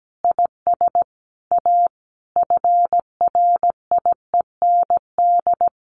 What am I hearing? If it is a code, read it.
Morse code, heard as a keyed tone, about 17 wpm, 700 Hz. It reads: IS A FRIEND
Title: Sentences from Top 1400 Words 17wpm